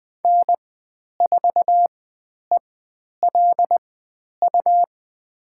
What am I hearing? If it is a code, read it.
Morse code, heard as a keyed tone, about 20 wpm, spaced as for 10 wpm, 700 Hz. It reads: N4ELU
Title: 109 Lesson 18g - Letter m - Call Signs 20wpm